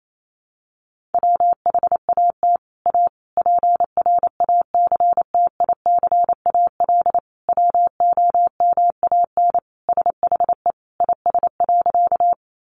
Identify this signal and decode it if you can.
Morse code, heard as a keyed tone, about 28 wpm, 700 Hz. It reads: WHAT A PRACTICAL WOMAN SHE IS.
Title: Sentences from Top 1400 Words 28wpm